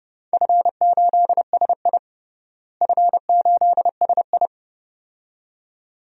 Morse code, heard as a keyed tone, about 29 wpm, 700 Hz. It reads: F8SI F8SI
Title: DXCC-100 Call Signs - 4 Characters - Repeated 2x - 30wpm